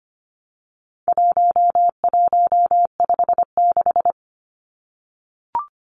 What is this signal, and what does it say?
Morse code, heard as a keyed tone, about 25 wpm, 700 Hz. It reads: 1156
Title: Numbers - 4-Digit 25wpm